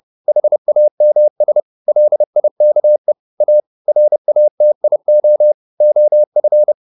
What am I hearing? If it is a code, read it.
Morse code, heard as a keyed tone, about 30 wpm, 600 Hz. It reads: HAMS LIKE A RATIO OF